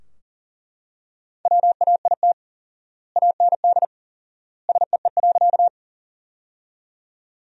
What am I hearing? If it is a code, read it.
Morse code, heard as a keyed tone, about 40 wpm, 700 Hz. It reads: WAIT AND SEE.
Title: 4x Word-Spacing - Sentences from Top 500 Words - 40wpm